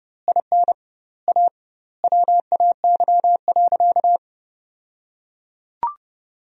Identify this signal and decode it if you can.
Morse code, heard as a keyed tone, about 30 wpm, 700 Hz. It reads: IN A WAY.
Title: 2x Word-Spacing - Sentences from Top 500 Words - 30wpm